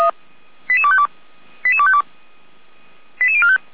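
Five-tone selective call (ZVEI), heard as quick runs of five tones, tones of about 70 ms. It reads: 88252 88252 88045
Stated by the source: ZVEI five-tone selective call received on an SDR, from feuerwehr-orlamuende.org (fire brigade Orlamünde).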